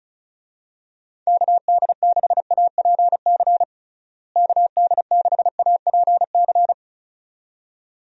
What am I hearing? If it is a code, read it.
Morse code, heard as a keyed tone, about 34 wpm, 700 Hz. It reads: KD6APC KD6APC
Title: USA Call Signs - 6 Characters - Repeated 2x - 35wpm